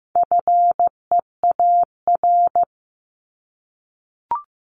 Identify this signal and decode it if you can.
Morse code, heard as a keyed tone, about 15 wpm, 700 Hz. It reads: FEAR
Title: Top 1751-2000 Words 15wpm